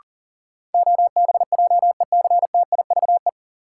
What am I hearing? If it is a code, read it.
Morse code, heard as a keyed tone, about 40 wpm, 700 Hz. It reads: OBJECTIVE
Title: Sentences from Top 1200 Words - New Word Drill Alt - 40wpm